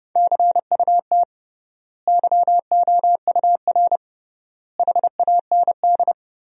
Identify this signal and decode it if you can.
Morse code, heard as a keyed tone, about 30 wpm, 700 Hz. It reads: CUT YOUR HAND
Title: 3x Word-Spacing - Sentences from Top 300 Words - 30wpm